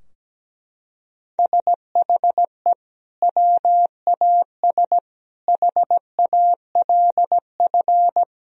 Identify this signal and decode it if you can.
Morse code, heard as a keyed tone, about 17 wpm, 700 Hz. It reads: SHE WAS HALF
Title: Sentences from Top 1000 Words - Review Part 2 - 17wpm